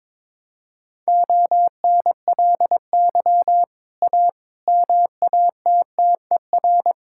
Morse code, heard as a keyed tone, about 22 wpm, 700 Hz. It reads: ONLY A MATTER
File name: practice / 22wpm / Sets of 6 Words - 22wpm